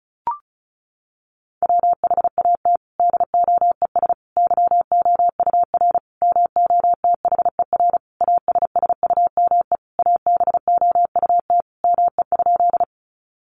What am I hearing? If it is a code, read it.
Morse code, heard as a keyed tone, about 34 wpm, 700 Hz. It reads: WHAT DOES YOUR MOTHER ASSUME ABOUT ME?
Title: Sentences from Top 1500 Words 35wpm